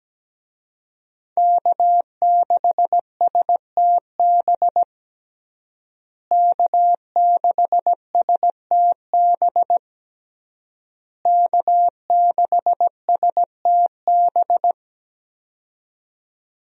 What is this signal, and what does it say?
Morse code, heard as a keyed tone, about 17 wpm, 700 Hz. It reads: K6STB K6STB K6STB
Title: USA Call Signs - 5 Characters - Repeated 3x - 17wpm